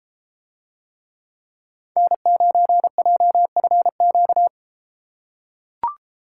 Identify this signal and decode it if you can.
Morse code, heard as a keyed tone, about 31 wpm, 700 Hz. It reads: N9JFQ